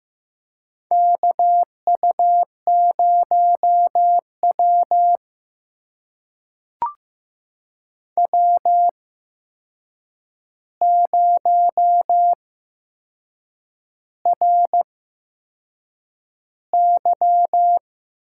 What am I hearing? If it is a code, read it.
Morse code, heard as a keyed tone, about 15 wpm, spaced as for 6 wpm, 700 Hz. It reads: KU0W W0RY
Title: USA Call Signs - 4 Characters - 8x Character Spacing - 15wpm